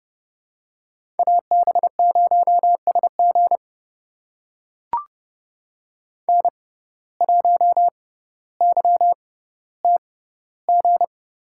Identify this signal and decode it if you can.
Morse code, heard as a keyed tone, about 30 wpm, spaced as for 15 wpm, 700 Hz. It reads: AB0SG N1YTG
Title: USA Call Signs - 5 Characters - 6x Character Spacing - 30wpm